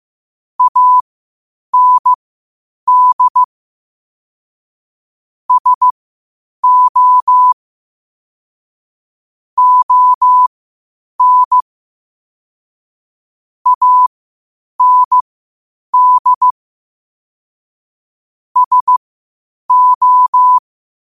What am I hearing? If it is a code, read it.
Morse code, heard as a keyed tone, about 15 wpm, spaced as for 8 wpm, 1000 Hz. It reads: AND SO ON AND SO
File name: 8wpm